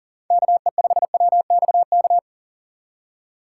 Morse code, heard as a keyed tone, about 40 wpm, 700 Hz. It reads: KE5WXK
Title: USA Call Signs - 4x Character Spacing - 40wpm